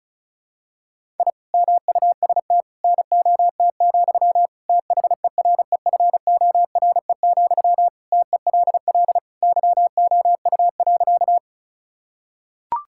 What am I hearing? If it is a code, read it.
Morse code, heard as a keyed tone, about 34 wpm, 700 Hz. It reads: I MUST NOT, THEREFORE, TELL YOU.
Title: Sentences from Top 1000 Words - Review Part 5 - 35wpm